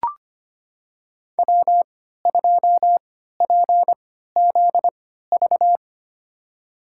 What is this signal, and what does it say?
Morse code, heard as a keyed tone, about 25 wpm, spaced as for 14 wpm, 700 Hz. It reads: W2PZV
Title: USA Call Signs - 5 Characters - 3x Character Spacing - 25wpm